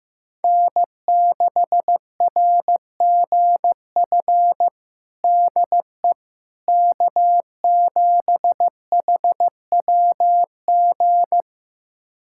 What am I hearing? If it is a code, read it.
Morse code, heard as a keyed tone, about 15 wpm, 700 Hz. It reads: N6RGF DE K7HWG
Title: USA Callsign DE Callsign - 5 Characters - 15wpm